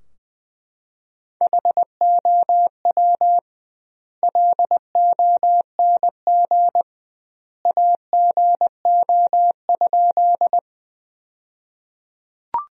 Morse code, heard as a keyed tone, about 21 wpm, 700 Hz. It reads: HOW LONG AGO?
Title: 2x Word-Spacing - Sentences from Top 500 Words - 20wpm